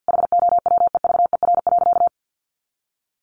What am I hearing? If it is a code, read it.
Morse code, heard as a keyed tone, about 50 wpm, 700 Hz. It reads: HOWEVER.
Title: Sentences from Top 600 Words 50wpm